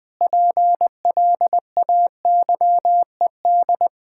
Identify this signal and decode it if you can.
Morse code, heard as a keyed tone, about 19 wpm, 700 Hz. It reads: PLAYED
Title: Sentences with 6-Word Phrases - 20wpm